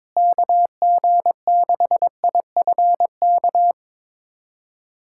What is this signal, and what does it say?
Morse code, heard as a keyed tone, about 21 wpm, 700 Hz. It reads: KG6IFK
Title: USA Callsign DE Callsign 22wpm